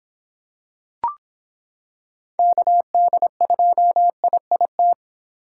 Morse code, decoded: KD2IIT

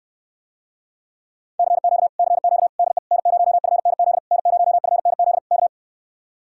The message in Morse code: CQ CQ DE N0FNZ N0FNZ K

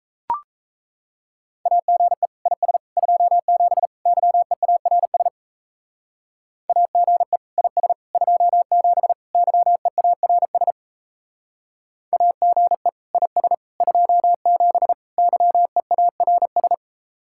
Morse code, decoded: AGE IS 27 YEARS AGE IS 27 YEARS AGE IS 27 YEARS

